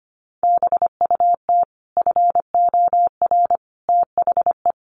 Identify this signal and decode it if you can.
Morse code, heard as a keyed tone, about 25 wpm, 700 Hz. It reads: BUT FOR THE
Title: Sets of 5 Words Encore 25wpm